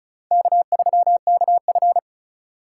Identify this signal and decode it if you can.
Morse code, heard as a keyed tone, about 34 wpm, 700 Hz. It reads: K3KF
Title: USA Call Signs - 4 Characters - 2x Character Spacing - 35wpm